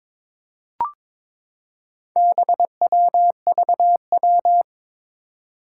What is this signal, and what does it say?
Morse code, heard as a keyed tone, about 22 wpm, 700 Hz. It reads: BWVW
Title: Four Challenging Letter-Number Combinations 22wpm